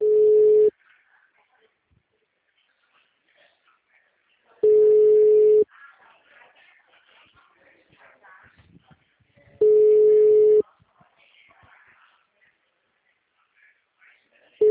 Ringback tone (Europe, 425 Hz).